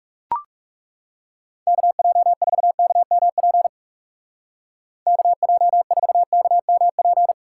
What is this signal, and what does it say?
Morse code, heard as a keyed tone, about 36 wpm, 700 Hz. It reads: KJ4KMP KJ4KMP
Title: Speed Racing - USA 2x3 Call Signs 30wpm